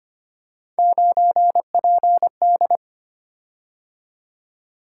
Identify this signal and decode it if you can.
Morse code, heard as a keyed tone, about 25 wpm, 700 Hz. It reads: 9PD
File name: Three Letter-Number Combinations - 25wpm